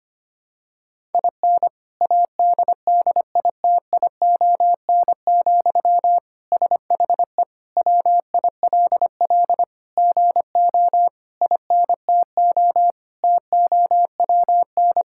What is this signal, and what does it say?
Morse code, heard as a keyed tone, about 25 wpm, 700 Hz. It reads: IN ADDITION, SHE WILL GO INTO TOWN
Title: Sentences from Top 1000 Words - Review Part 6 - 25wpm